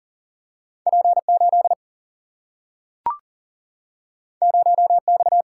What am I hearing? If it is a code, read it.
Morse code, heard as a keyed tone, about 40 wpm, 700 Hz. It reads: P 8 0 X
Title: Single-Letter Single-Number 40wpm